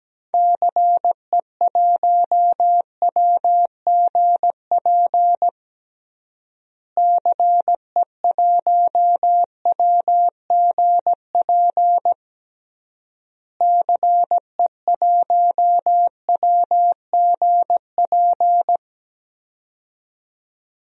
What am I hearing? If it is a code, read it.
Morse code, heard as a keyed tone, about 17 wpm, 700 Hz. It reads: CE1WGP CE1WGP CE1WGP
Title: DXCC-100 Call Signs - 6 Characters - Repeated 3x - 17wpm